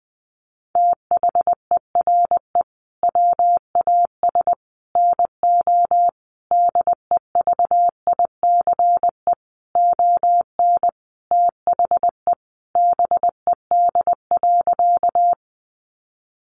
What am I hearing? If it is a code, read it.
Morse code, heard as a keyed tone, about 20 wpm, 700 Hz. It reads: THERE WAS NO DEVICE ON THE BED.
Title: Sentences from Top 1500 Words 20wpm